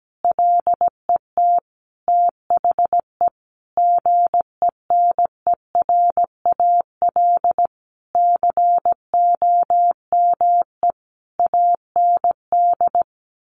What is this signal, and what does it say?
Morse code, heard as a keyed tone, about 17 wpm, 700 Hz. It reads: LET THE GENERAL COME AND
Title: Sentences from Top 1400 Words 17wpm